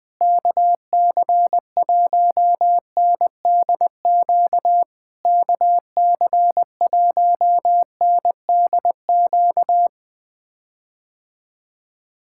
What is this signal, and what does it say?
Morse code, heard as a keyed tone, about 20 wpm, 700 Hz. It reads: KC1NDQ KC1NDQ
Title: POTA QSO Practice - 20wpm